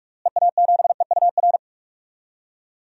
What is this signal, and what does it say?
Morse code, heard as a keyed tone, about 42 wpm, 700 Hz. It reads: EA7EUR
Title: DXCC-100 Call Signs - 6 Characters - Repeated 2x - 45wpm